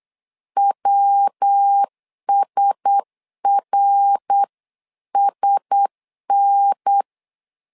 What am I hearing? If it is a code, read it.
Morse code, heard as a keyed tone, about 8 wpm, 800 Hz. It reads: WSRSN
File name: LEZIONI / 4 Q1S 8wpm 2710